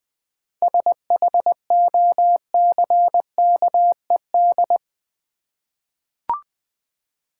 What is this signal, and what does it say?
Morse code, heard as a keyed tone, about 20 wpm, 700 Hz. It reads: SHOCKED
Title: 7 Letter Words - First 3 Letters Given - 20wpm